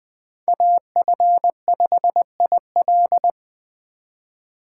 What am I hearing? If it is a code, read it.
Morse code, heard as a keyed tone, about 20 wpm, 700 Hz. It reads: AF5IL
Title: Speed Racing - USA 2x2 Call Signs 17wpm